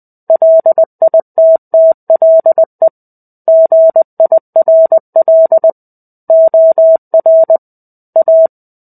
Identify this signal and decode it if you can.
Morse code, heard as a keyed tone, about 20 wpm, 650 Hz. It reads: LITTLE GIRL OR A
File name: 05-Chapter-05 20 WPM